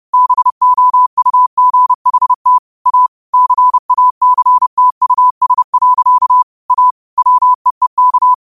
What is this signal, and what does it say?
Morse code, heard as a keyed tone, about 30 wpm, 1000 Hz. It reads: BOUGHT A CACTUS. A WEEK
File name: qotdfiles / 30wpm